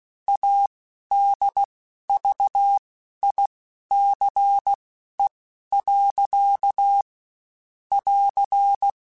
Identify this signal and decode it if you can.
Morse code, heard as a keyed tone, about 16 wpm, spaced as for 11 wpm, 800 Hz. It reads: ADVICE. <AR>
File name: Quotes Spaced / Quotes Spaced-mp3s / Advice sp @15